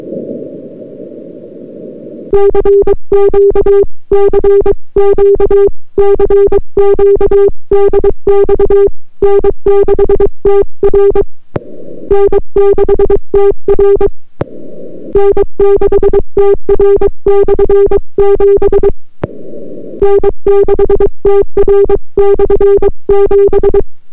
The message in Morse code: CQ CQ CQ DX N6TR N6TR N6TR/7 N6TR/7